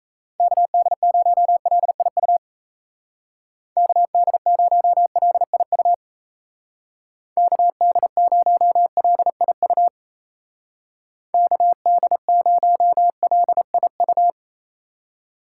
Morse code, heard as a keyed tone, about 35 wpm, 700 Hz. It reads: KD0LIU KD0LIU KD0LIU KD0LIU